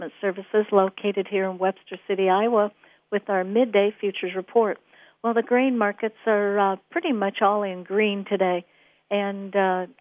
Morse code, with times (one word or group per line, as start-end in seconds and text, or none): none